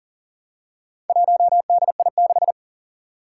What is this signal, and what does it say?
Morse code, heard as a keyed tone, about 37 wpm, 700 Hz. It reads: 1DI6
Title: Four Challenging Letter-Number Combinations 40wpm